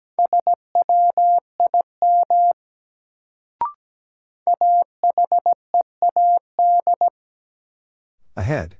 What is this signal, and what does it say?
Morse code, heard as a keyed tone, about 17 wpm, 700 Hz. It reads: SWIM AHEAD
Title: Sentences from Top 1000 Words - Review Part 6 - New Word Drill - 17wpm